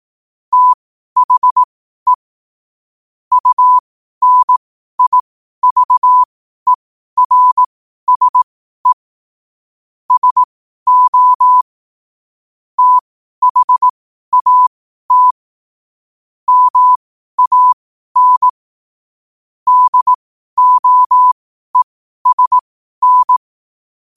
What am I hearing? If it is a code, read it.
Morse code, heard as a keyed tone, about 18 wpm, spaced as for 12 wpm, 1000 Hz. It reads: THE UNIVERSE SO THAT MAN DOESN
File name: qotdfiles / 12wpm